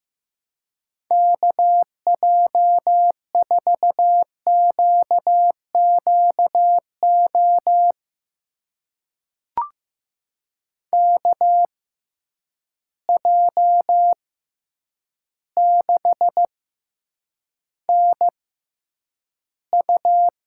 Morse code, heard as a keyed tone, about 15 wpm, spaced as for 8 wpm, 700 Hz. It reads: KJ4QQO KJ6NU
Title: USA Call Signs - 6 Characters - 6x Character Spacing - 15wpm